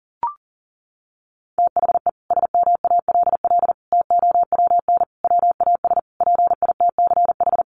Morse code, heard as a keyed tone, about 37 wpm, 700 Hz. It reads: THE SMALL TOWN WAS PITCH
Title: Sentences from Top 1300 Words 40wpm